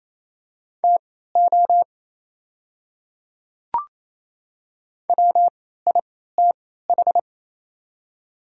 Morse code, heard as a keyed tone, about 27 wpm, spaced as for 16 wpm, 700 Hz. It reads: TO WITH
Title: 3x Character Spacing - Top 100 Words - 28wpm